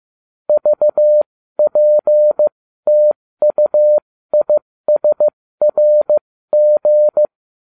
Morse code, heard as a keyed tone, about 15 wpm, spaced as for 13 wpm, 600 Hz. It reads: VPTUISRG